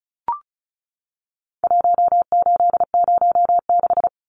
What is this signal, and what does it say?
Morse code, heard as a keyed tone, about 34 wpm, 700 Hz. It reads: 1806
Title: Numbers - 4-Digit 35wpm